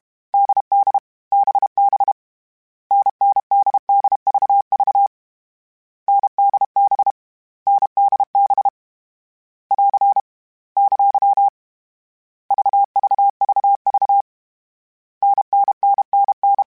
Morse code, heard as a keyed tone, about 32 wpm, 800 Hz. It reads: DD BB NNDDVV NDB NDB <AR> ! VVVV NNNNN
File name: Variable Speed / Variable Speed-mp3s / GZ GZ7 sp @30